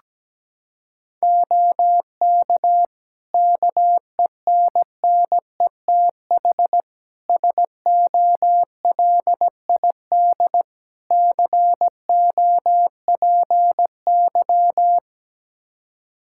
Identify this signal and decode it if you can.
Morse code, heard as a keyed tone, about 17 wpm, 700 Hz. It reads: OK KENNETH SOLID COPY
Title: Practice QSO - Exchange 3 - More Info - 17wpm